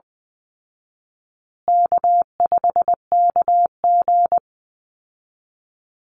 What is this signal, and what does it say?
Morse code, heard as a keyed tone, about 20 wpm, 700 Hz. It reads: K5KG
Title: Easy Call Signs Repeated 3x - 20wpm